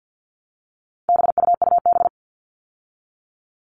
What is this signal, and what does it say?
Morse code, heard as a keyed tone, about 50 wpm, 700 Hz. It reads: BUUB
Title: All Cut Numbers - 4-Digit 50wpm